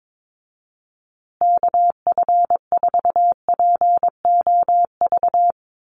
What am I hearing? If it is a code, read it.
Morse code, heard as a keyed tone, about 22 wpm, 700 Hz. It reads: KF4POV
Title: USA Call Signs Repeated 3x 22wpm